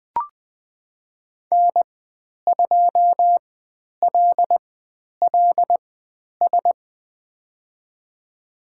Morse code, heard as a keyed tone, about 20 wpm, spaced as for 10 wpm, 700 Hz. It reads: N2LLS
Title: 095 Lesson 16g - Letter c - Call Signs 20wpm